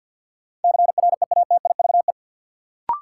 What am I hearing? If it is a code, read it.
Morse code, heard as a keyed tone, about 52 wpm, 700 Hz. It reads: CREATIVE
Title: Sentences from Top 1000 Words - Review Part 5 - New Word Drill - 50wpm